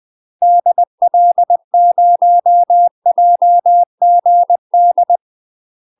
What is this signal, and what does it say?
Morse code, heard as a keyed tone, about 20 wpm, 700 Hz. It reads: DL0JGD
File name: UK MASTER QSOs TRAINING / 20WPM / UK MASTER QSOs 0051